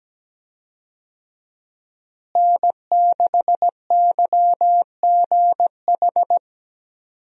Morse code, heard as a keyed tone, about 17 wpm, 700 Hz. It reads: N6YGH
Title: USA Call Signs - 5 Characters - Repeated 3x - 17wpm